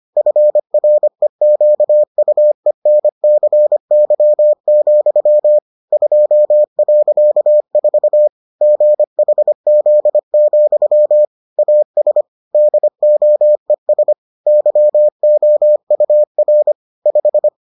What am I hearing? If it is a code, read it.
Morse code, heard as a keyed tone, about 25 wpm, 600 Hz. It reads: FREQUENCY, 2.4 GHZ, AS DOES YOUR 5